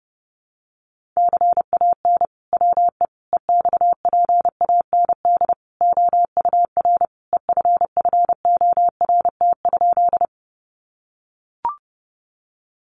Morse code, heard as a keyed tone, about 30 wpm, 700 Hz. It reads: CAN WE EXPAND OUR EFFORT?
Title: Sentences from Top 1400 Words 30wpm